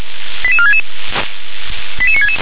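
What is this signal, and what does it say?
Five-tone selective call (ZVEI), heard as quick runs of five tones, tones of about 80 ms. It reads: 88466 88966
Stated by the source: ZVEI five-tone call 88461 received by SDR.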